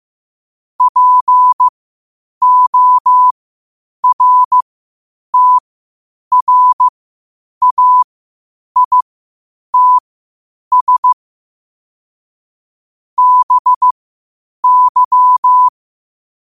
Morse code, heard as a keyed tone, about 15 wpm, spaced as for 8 wpm, 1000 Hz. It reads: PORTRAITS BY
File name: qotdfiles / 8wpm